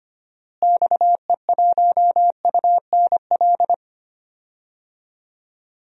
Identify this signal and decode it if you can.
Morse code, heard as a keyed tone, about 25 wpm, 700 Hz. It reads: XE1UNL
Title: DXCC-100 Call Signs - 6 Characters - Repeated 3x - 25wpm